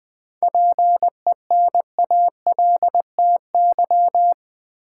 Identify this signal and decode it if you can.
Morse code, heard as a keyed tone, about 20 wpm, 700 Hz. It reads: PENALTY
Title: 7 Letter Words - First 3 Letters Given - 20wpm